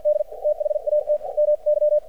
Morse code, heard as a keyed tone, about 54 wpm, spaced as for 46 wpm, 600 Hz. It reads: D THAT MY
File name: archived / CRSnet